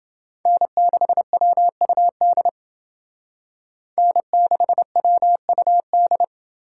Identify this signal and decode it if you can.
Morse code, heard as a keyed tone, about 28 wpm, 700 Hz. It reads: N6WUD N6WUD